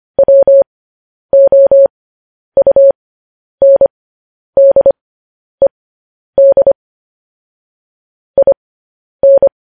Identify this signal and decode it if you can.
Morse code, heard as a keyed tone, about 25 wpm, spaced as for 10 wpm, 550 Hz. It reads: WOUNDED IN
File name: podcast / News headlines-1x-25x10